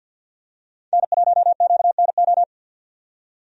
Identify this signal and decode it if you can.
Morse code, heard as a keyed tone, about 50 wpm, 700 Hz. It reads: N1QNO
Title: USA Call Signs - 5 Characters - 50wpm